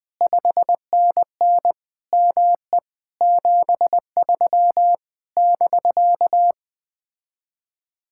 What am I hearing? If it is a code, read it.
Morse code, heard as a keyed tone, about 20 wpm, 700 Hz. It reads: 5NN ME 73 <BK>
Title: Speed-Racing Common QSO Phrases - 17wpm